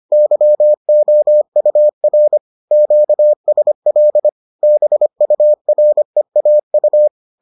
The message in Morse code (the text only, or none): YOUR QSL BUREAU